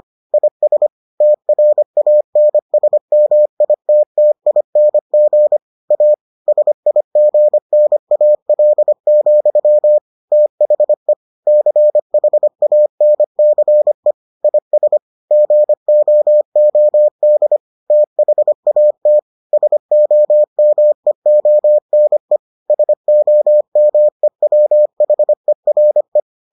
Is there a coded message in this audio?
Yes: Morse code from 0.3 to 26.2 s, heard as a keyed tone, about 25 wpm, 600 Hz. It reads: IS TRANSMITTING A SIGNAL, THE CHANCE IS GOOD THAT SOMEONE SOMEWHERE